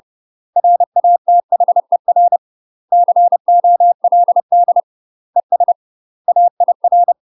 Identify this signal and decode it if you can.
Morse code, heard as a keyed tone, about 30 wpm, 700 Hz. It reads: RATHER COLD ES AIR
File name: UK MASTER QSOs TRAINING / 30WPM / UK MASTER QSOs 0023